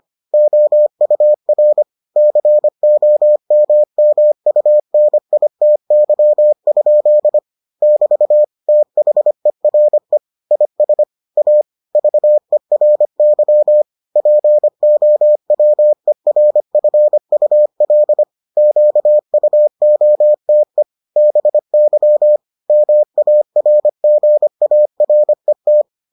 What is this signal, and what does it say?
Morse code, heard as a keyed tone, about 25 wpm, 600 Hz. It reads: OUR COMMUNITY? <BT> THERE IS A VERY POWERFUL QUOTE BY MARGARET